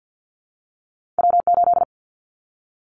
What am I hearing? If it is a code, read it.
Morse code, heard as a keyed tone, about 50 wpm, 700 Hz. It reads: W8
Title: Callsign Prefixes for Top 25 DXCC - 50wpm